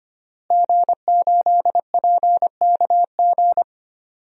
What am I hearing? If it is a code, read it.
Morse code, heard as a keyed tone, about 25 wpm, 700 Hz. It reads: G8PKG
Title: DXCC-100 Call Signs - 5 Characters - Repeated 3x - 25wpm